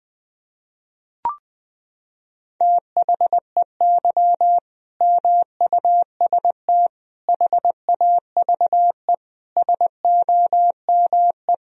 Morse code, heard as a keyed tone, about 21 wpm, 700 Hz. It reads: THEY MUST HAVE SOME
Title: Sentences from Top 1000 Words - Review Part 3 - 20wpm